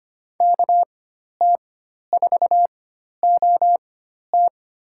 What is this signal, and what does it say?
Morse code, heard as a keyed tone, about 25 wpm, spaced as for 12 wpm, 700 Hz. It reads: KT4OT
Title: USA Call Signs - 4x Character Spacing - 25wpm